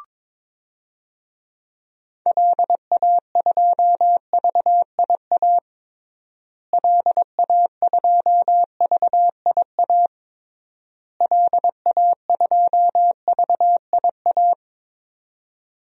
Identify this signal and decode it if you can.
Morse code, heard as a keyed tone, about 22 wpm, 700 Hz. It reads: LA2VIA LA2VIA LA2VIA
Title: DXCC-100 Call Signs - 6 Characters - Repeated 3x - 22wpm